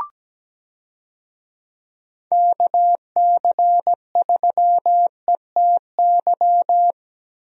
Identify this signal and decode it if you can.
Morse code, heard as a keyed tone, about 17 wpm, 700 Hz. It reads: KC3ETY
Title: USA Call Signs - 6 Characters - Repeated 2x - 17wpm